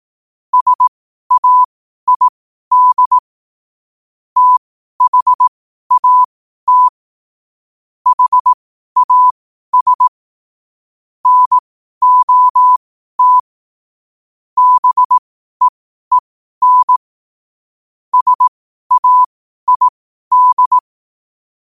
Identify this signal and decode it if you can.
Morse code, heard as a keyed tone, about 18 wpm, spaced as for 12 wpm, 1000 Hz. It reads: SAID THAT HAS NOT BEEN SAID
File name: qotdfiles / 12wpm